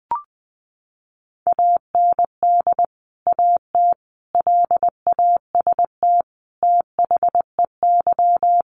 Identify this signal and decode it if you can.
Morse code, heard as a keyed tone, about 20 wpm, 700 Hz. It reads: AND AT LAST THEY
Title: Sentences from Top 1200 Words 20wpm